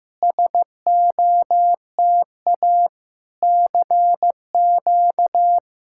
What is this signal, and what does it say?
Morse code, heard as a keyed tone, about 15 wpm, 700 Hz. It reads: SOTA CQ
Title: SOTA QSO Practice 15wpm